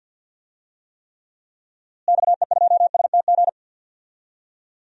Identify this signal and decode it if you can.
Morse code, heard as a keyed tone, about 52 wpm, 700 Hz. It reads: XE2STG
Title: DXCC-100 Call Signs - 6 Characters - Repeated 2x - 50wpm